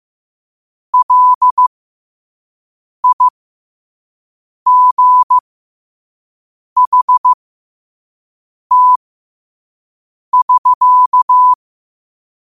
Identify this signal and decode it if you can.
Morse code, heard as a keyed tone, about 15 wpm, spaced as for 5 wpm, 1000 Hz. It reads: LIGHT<SK>
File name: qotdfiles / 5wpm